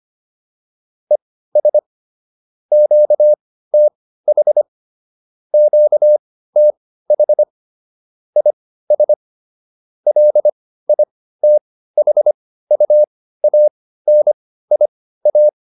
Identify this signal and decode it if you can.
Morse code, heard as a keyed tone, about 25 wpm, spaced as for 15 wpm, 600 Hz. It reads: ES QTH QTH IS LITHUANIA